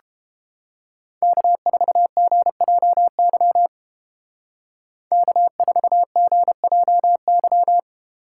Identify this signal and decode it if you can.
Morse code, heard as a keyed tone, about 29 wpm, 700 Hz. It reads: K4GJY K4GJY